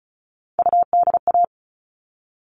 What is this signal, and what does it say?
Morse code, heard as a keyed tone, about 34 wpm, 700 Hz. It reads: UDA
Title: All Cut Numbers - 3-Digit 35wpm